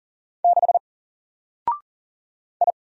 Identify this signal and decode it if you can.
Morse code, heard as a keyed tone, about 40 wpm, 700 Hz. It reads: 6 I
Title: Single Letter-Number Alt 40wpm